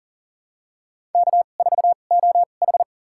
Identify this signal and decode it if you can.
Morse code, heard as a keyed tone, about 38 wpm, spaced as for 28 wpm, 700 Hz. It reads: K4OH
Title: USA Call Signs - 4 Characters - 2x Character Spacing - 40wpm